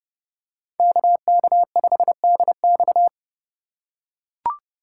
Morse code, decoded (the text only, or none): KK5DX